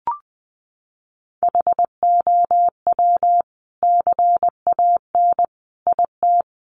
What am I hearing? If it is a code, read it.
Morse code, heard as a keyed tone, about 20 wpm, 700 Hz. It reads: HOW CAN IT
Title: Sentences from Top 1300 Words 20wpm